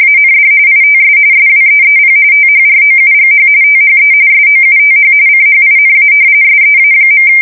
RTTY, about 45.45 baud. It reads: AND ITS SWEEP THROUGH THE SOUTH CHINA SEA.
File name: fsk170 / battleshipnj